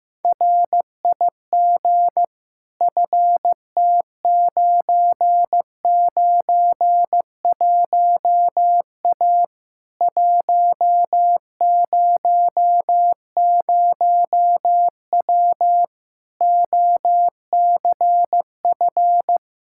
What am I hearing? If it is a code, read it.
Morse code, heard as a keyed tone, about 15 wpm, 700 Hz. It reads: RIG FT991A 100W OCF
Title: K1USN SST QSO Practice 15wpm